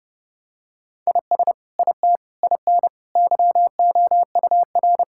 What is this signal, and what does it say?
Morse code, heard as a keyed tone, about 30 wpm, 700 Hz. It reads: IS IT IN YOUR
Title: Sentences from Top 1000 Words - Review Part 5 - 30wpm